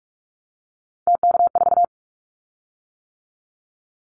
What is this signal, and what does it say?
Morse code, heard as a keyed tone, about 46 wpm, 700 Hz. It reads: TK4
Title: Callsign Prefixes for Top 100 DXCC - 45wpm